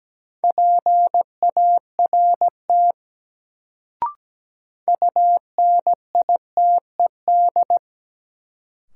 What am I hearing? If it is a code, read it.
Morse code, heard as a keyed tone, about 17 wpm, 700 Hz. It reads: PART UNITED
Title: Sets of 2 Words - New Word Drill - 17wpm